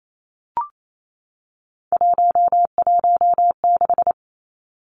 Morse code, heard as a keyed tone, about 28 wpm, 700 Hz. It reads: 116
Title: Numbers - 3-Digit 28wpm